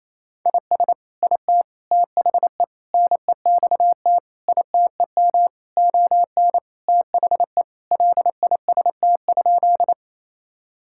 Morse code, heard as a keyed tone, about 28 wpm, 700 Hz. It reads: IS IT THE NEXT ITEM ON THE LIST?
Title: Sentences from Top 1000 Words - Review Part 6 - 28wpm